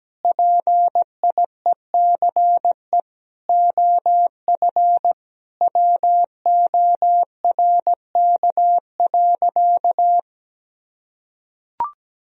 Morse code, heard as a keyed tone, about 17 wpm, 700 Hz. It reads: PIECE OF WORK.
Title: Sentences from Top 1000 Words - Review Part 2 - 17wpm